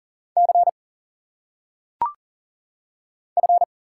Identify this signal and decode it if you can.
Morse code, heard as a keyed tone, about 38 wpm, 700 Hz. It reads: C F
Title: Single Letters - Phonetic Pronunciation 40wpm